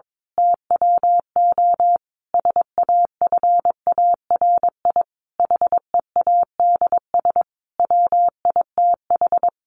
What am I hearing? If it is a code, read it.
Morse code, heard as a keyed tone, about 22 wpm, 700 Hz. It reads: TWO SAFARI HEADS WITH
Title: A Sound Of Thunder 22wpm